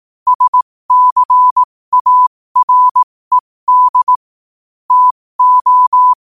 Morse code, heard as a keyed tone, about 18 wpm, spaced as for 15 wpm, 1000 Hz. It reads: SCARED TO
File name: qotdfiles / 15wpm